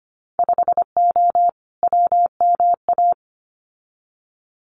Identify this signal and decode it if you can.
Morse code, heard as a keyed tone, about 25 wpm, 700 Hz. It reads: 5O WMA
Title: Winter Field Day Exchanges Encore 25wpm